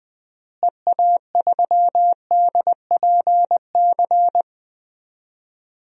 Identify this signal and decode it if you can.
Morse code, heard as a keyed tone, about 20 wpm, 700 Hz. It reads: EA3DPC
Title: DXCC-100 Call Signs - 6 Characters - Repeated 3x - 20wpm